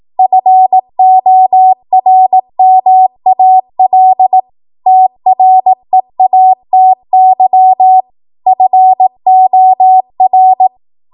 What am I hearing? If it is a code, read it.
Morse code, heard as a keyed tone, about 18 wpm, 750 Hz. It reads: FORMAL TREATY FOR
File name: Morse / 18WPM